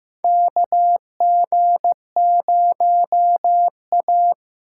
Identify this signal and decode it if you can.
Morse code, heard as a keyed tone, about 15 wpm, 700 Hz. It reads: KG0A
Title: USA Callsign DE Callsign - 4 Characters - 15wpm